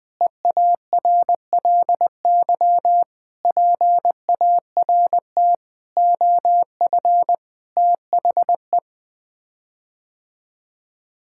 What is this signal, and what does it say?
Morse code, heard as a keyed tone, about 20 wpm, 700 Hz. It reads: EARLY PART OF THE
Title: Sets of 6 Words - 20wpm